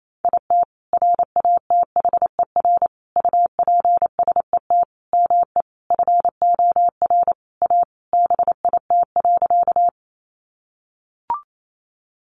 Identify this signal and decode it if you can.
Morse code, heard as a keyed tone, about 28 wpm, 690 Hz. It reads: IT RATHER UPSET ME FOR A BIT.
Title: Sentences from Top 900 Words 28wpm